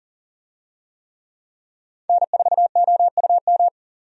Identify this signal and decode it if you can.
Morse code, heard as a keyed tone, about 40 wpm, 700 Hz. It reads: N4OUM
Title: USA Call Signs - 5 Characters - 40wpm